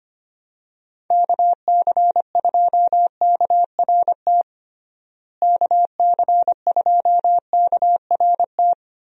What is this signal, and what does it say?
Morse code, heard as a keyed tone, about 25 wpm, 700 Hz. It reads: KC2KRT KC2KRT
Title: USA Call Signs - 6 Characters - Repeated 2x - 25wpm